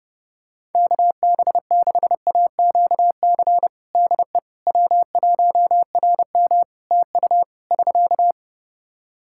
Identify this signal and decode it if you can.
Morse code, heard as a keyed tone, about 32 wpm, 700 Hz. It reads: KB6AQC DE W1RM TU <SK>
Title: Practice QSO - Exchange 4 - Ending - 30wpm